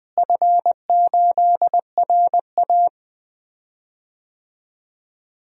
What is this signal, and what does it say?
Morse code, heard as a keyed tone, about 20 wpm, 700 Hz. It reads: F8RA
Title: DXCC-100 Call Signs - 4 Characters - Repeated 3x - 20wpm